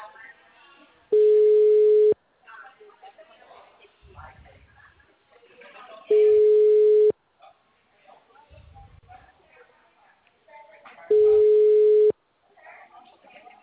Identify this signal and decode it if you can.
Ringback tone (Europe, 425 Hz), with speech.